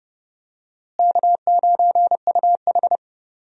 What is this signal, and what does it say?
Morse code, heard as a keyed tone, about 32 wpm, 700 Hz. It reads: K9UH
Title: USA Call Signs - 4 Characters - Repeated 2x - 30wpm